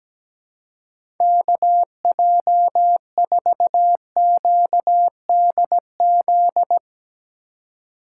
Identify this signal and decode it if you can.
Morse code, heard as a keyed tone, about 17 wpm, 700 Hz. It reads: KJ4QDZ
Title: USA Call Signs - 6 Characters - 4x Character Spacing - 17wpm